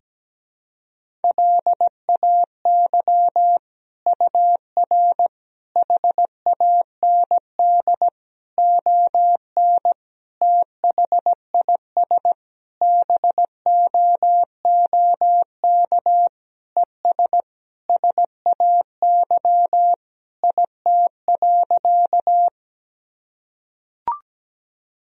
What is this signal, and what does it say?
Morse code, heard as a keyed tone, about 17 wpm, 700 Hz. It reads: LAY UR HAND ON THIS BOOK ES SAY IT.
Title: Sentences from Top 400 Words and Abbreviations 17wpm